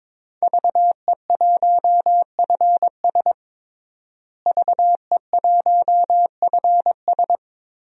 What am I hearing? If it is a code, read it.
Morse code, heard as a keyed tone, about 22 wpm, 700 Hz. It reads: VE1FS VE1FS
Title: DXCC-100 Call Signs - 5 Characters - Repeated 2x - 22wpm